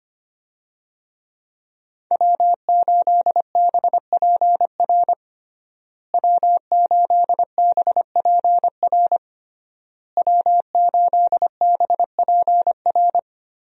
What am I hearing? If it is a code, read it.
Morse code, heard as a keyed tone, about 25 wpm, 700 Hz. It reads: W8BPR W8BPR W8BPR
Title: USA Call Signs - 5 Characters - Repeated 3x - 25wpm